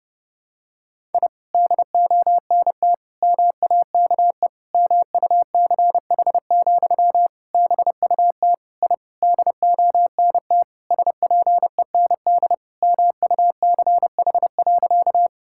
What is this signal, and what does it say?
Morse code, heard as a keyed tone, about 32 wpm, 700 Hz. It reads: I DONT MAKE MUCH, BUT I DONT SPEND MUCH.
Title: Sentences from Top 1000 Words - Review Part 5 - 30wpm